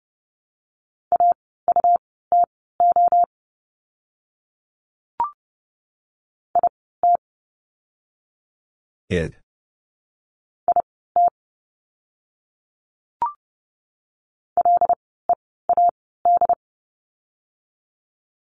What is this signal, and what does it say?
Morse code, heard as a keyed tone, about 30 wpm, spaced as for 17 wpm, 700 Hz. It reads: AUTO IT IT LEAD
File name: Lesson 17f - Letter u - Words 30wpm